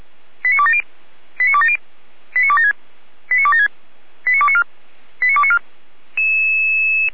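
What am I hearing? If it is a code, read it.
Five-tone selective call (ZVEI), heard as quick runs of five tones, tones of about 70 ms. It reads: 89270 89270 89276 89276 89294 89294
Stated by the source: Received by SDR on a Feuerwehr Orlamünde server.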